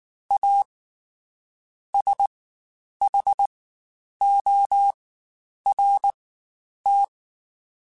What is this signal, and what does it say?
Morse code, heard as a keyed tone, about 19 wpm, spaced as for 9 wpm, 800 Hz. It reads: A SHORT
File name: Quotes Spaced / Quotes Spaced-mp3s / Temper sp @18